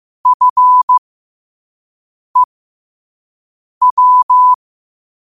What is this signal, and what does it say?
Morse code, heard as a keyed tone, about 15 wpm, spaced as for 5 wpm, 1000 Hz. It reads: FEW
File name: qotdfiles / 5wpm